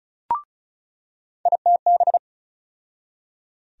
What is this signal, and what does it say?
Morse code, heard as a keyed tone, about 34 wpm, 700 Hz. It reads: ITB
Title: USA Call Sign Postfixes - 3 Letters 35wpm